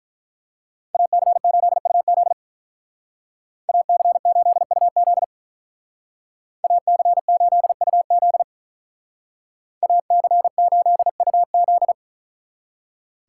Morse code, heard as a keyed tone, about 43 wpm, 700 Hz. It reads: AC8UZ AC8UZ AC8UZ AC8UZ
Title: Speed Racing - USA 2x2 Call Signs 35wpm